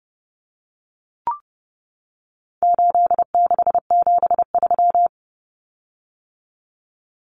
Morse code, heard as a keyed tone, about 30 wpm, 700 Hz. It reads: 8673
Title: Common Cut Numbers - 4-Digit 30wpm